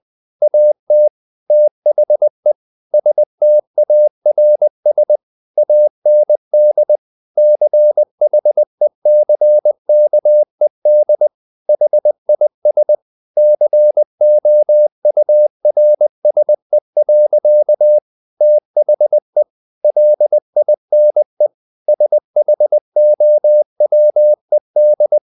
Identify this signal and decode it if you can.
Morse code, heard as a keyed tone, about 20 wpm, 600 Hz. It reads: AT THE STARS AND CHECKED HIS COURSE. THE LINE SHOWED
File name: the old man and the sea / the old man and the sea part 1 20wpm